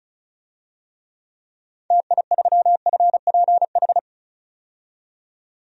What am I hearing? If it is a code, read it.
Morse code, heard as a keyed tone, about 34 wpm, 700 Hz. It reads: TI3FPH
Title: DXCC-100 Call Signs - 6 Characters - 35wpm